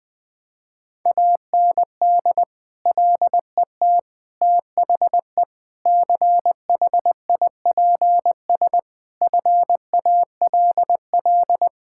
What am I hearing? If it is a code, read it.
Morse code, heard as a keyed tone, about 20 wpm, 700 Hz. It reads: AND LET THE CHIPS FALL
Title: Sentences with 8-Word Phrases - Spoken Beginning - 20wpm